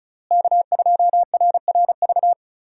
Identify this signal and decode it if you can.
Morse code, heard as a keyed tone, about 34 wpm, 700 Hz. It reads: K2RRV
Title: USA Call Signs - 5 Characters - Repeated 3x - 35wpm